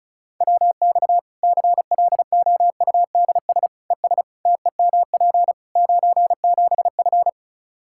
Morse code, heard as a keyed tone, about 36 wpm, 700 Hz. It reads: WX CLOUDS ES TEMP 97F
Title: Practice QSO - Exchange 2 - Info - 35wpm